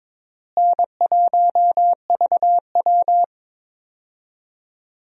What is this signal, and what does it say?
Morse code, heard as a keyed tone, about 22 wpm, 700 Hz. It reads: N1VW
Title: USA Call Signs - 4 Characters - Repeated 2x - 22wpm